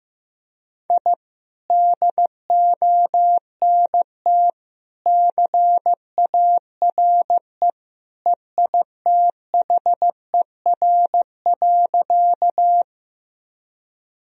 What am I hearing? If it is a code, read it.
Morse code, heard as a keyed tone, about 15 wpm, 700 Hz. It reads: I DONT CARE EITHER.
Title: Sentences from Top 1000 Words - Review Part 4 - 15wpm